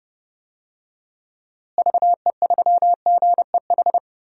Morse code, heard as a keyed tone, about 30 wpm, 700 Hz. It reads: VE3GEH
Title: DXCC-100 Call Signs - 6 Characters - Repeated 3x - 30wpm